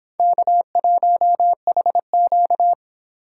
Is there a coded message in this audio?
Yes: Morse code from 0.2 to 2.7 s, heard as a keyed tone, about 29 wpm, 700 Hz. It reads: K1HQ